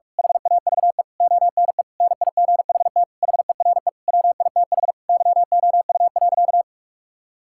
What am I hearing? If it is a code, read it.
Morse code, heard as a keyed tone, about 41 wpm, 700 Hz. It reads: HAVE ONE NIGHT HERE WITH YOU.
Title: Sentences from Top 1600 Words 45wpm